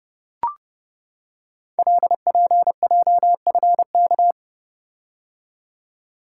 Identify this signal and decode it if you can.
Morse code, heard as a keyed tone, about 30 wpm, 700 Hz. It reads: LPJFK
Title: Code Groups - 5 Letters - 30wpm